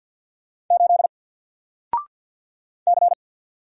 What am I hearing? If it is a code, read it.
Morse code, heard as a keyed tone, about 48 wpm, 700 Hz. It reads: 8 C
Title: Single Letter-Number Alt 50wpm